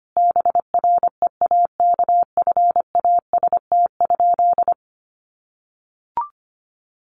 Morse code, heard as a keyed tone, about 25 wpm, 700 Hz. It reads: BREAKFAST?
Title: Sentences from Top 1500 Words 25wpm